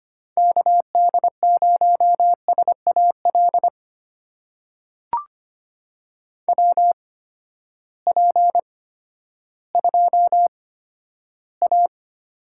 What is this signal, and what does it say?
Morse code, heard as a keyed tone, about 25 wpm, spaced as for 12 wpm, 700 Hz. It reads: KD0SAL WP2A
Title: USA Call Signs - 6 Characters - 8x Character Spacing - 25wpm